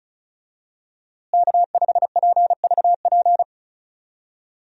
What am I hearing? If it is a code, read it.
Morse code, heard as a keyed tone, about 36 wpm, 700 Hz. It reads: K5PVP
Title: USA Call Signs - 5 Characters - Repeated 3x - 35wpm